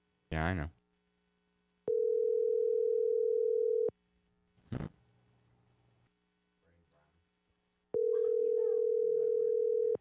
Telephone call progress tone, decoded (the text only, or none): ringback tone